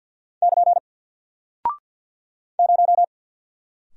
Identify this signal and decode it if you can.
Morse code, heard as a keyed tone, about 49 wpm, 700 Hz. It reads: <KN> 0
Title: Single Character (Letter:Number:Punctuation:Prosign) Alt 50wpm